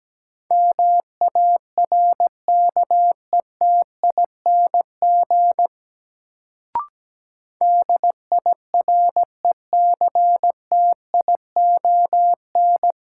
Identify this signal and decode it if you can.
Morse code, heard as a keyed tone, about 17 wpm, 700 Hz. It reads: MARKETING DIRECTION
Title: Sentences from Top 1000 Words - Review Part 5 - New Word Drill - 17wpm